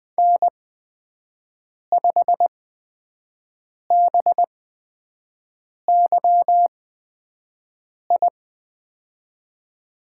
Morse code, heard as a keyed tone, about 20 wpm, spaced as for 5 wpm, 700 Hz. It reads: N5BYI
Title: USA Call Signs - 5 Characters - 8x Character Spacing - 20wpm